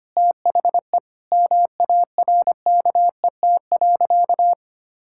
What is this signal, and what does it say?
Morse code, heard as a keyed tone, about 25 wpm, 700 Hz. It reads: THE MARKET.
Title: Sentences from Top 1000 Words - Review Part 6 - 25wpm